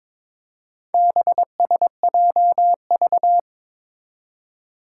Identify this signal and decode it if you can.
Morse code, heard as a keyed tone, about 22 wpm, 700 Hz. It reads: BSJV
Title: Four Challenging Letter-Number Combinations 22wpm